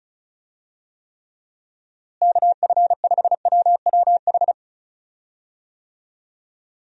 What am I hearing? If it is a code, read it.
Morse code, heard as a keyed tone, about 36 wpm, 700 Hz. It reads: KF5WWH
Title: USA Call Signs - 6 Characters - Repeated 2x - 35wpm